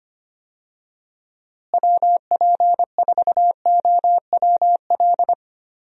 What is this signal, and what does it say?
Morse code, heard as a keyed tone, about 25 wpm, 700 Hz. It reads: WP4OWL
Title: USA Call Signs - 6 Characters - Repeated 3x - 25wpm